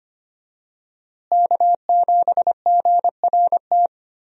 Morse code, heard as a keyed tone, about 25 wpm, 700 Hz. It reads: K7GRT
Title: USA Call Signs - 5 Characters - Repeated 3x - 25wpm